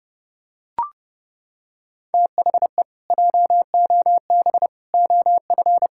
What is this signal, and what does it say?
Morse code, heard as a keyed tone, about 30 wpm, 700 Hz. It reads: THE JOB OF
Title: Pangram Challenge v2 - 30wpm